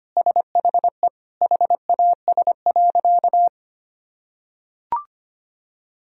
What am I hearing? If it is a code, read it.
Morse code, heard as a keyed tone, about 25 wpm, 700 Hz. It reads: SHE HAS.
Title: Sentences from Top 500 Words and Abbreviations 25wpm